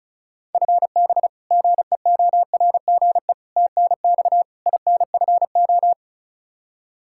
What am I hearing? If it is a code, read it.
Morse code, heard as a keyed tone, about 36 wpm, 700 Hz. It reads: FB GEORGE TNX INFO
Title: K1USN SST QSO Practice 35wpm